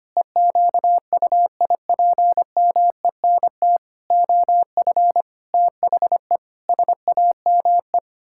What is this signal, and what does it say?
Morse code, heard as a keyed tone, about 25 wpm, 700 Hz. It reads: EQUIPMENT OF THE SAME
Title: Pangram Challenge v2 - 25wpm